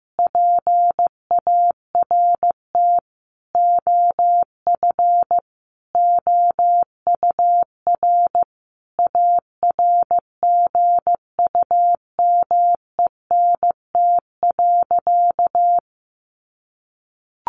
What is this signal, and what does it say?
Morse code, heard as a keyed tone, about 14 wpm, 700 Hz. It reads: PART OF OUR ARGUMENT.
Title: Sentences from Top 1400 Words 15wpm